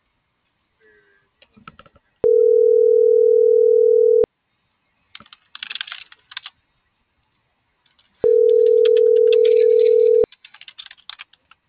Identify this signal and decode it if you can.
Ringback tone (North America), 440 + 480 Hz.